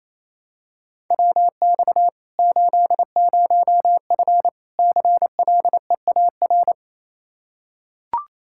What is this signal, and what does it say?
Morse code, heard as a keyed tone, about 28 wpm, 700 Hz. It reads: WX 80F CLEAR
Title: Practice QSO - Exchange 2 - Info - 28wpm